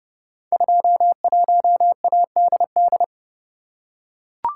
Morse code, heard as a keyed tone, about 31 wpm, 700 Hz. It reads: 21ADD
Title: Code Groups - 5 Letter:Numbers - 30wpm